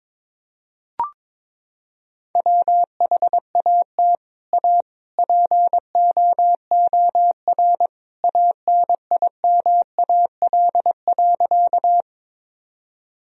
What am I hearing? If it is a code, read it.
Morse code, heard as a keyed tone, about 23 wpm, 700 Hz. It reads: WHAT A POOR ANIMAL.
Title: Sentences from Top 1000 Words - Review Part 1 - 22wpm